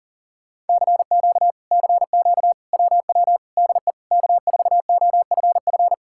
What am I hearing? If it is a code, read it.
Morse code, heard as a keyed tone, about 52 wpm, 700 Hz. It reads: CQ CQ WW DE K4OFF